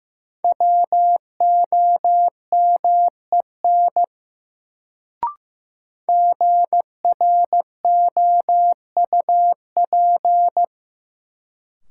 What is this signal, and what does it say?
Morse code, heard as a keyed tone, about 15 wpm, 700 Hz. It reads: WOMEN GROUP
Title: Sets of 3 Words - New Word Drill - 15wpm